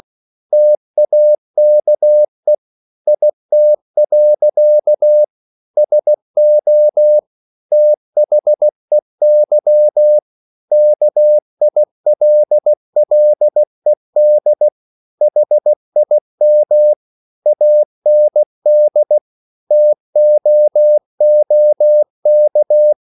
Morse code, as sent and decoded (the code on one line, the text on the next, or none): - .- -.- . / .. - .-.-.- / ... --- / - .... -.-- / -.- .. .-.. .-.. -.. / .... .. -- / .- -. -.. / - --- --- -.-
TAKE IT. SO THEY KILLED HIM AND TOOK